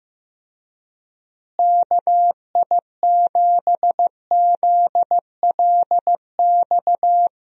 Morse code, heard as a keyed tone, about 15 wpm, 700 Hz. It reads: KI7ZLX
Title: USA Call Signs - 6 Characters - Repeated 2x - 15wpm